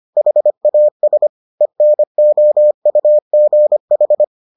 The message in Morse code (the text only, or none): HAS ENOUGH